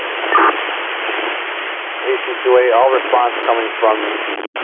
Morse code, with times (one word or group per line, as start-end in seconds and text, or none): none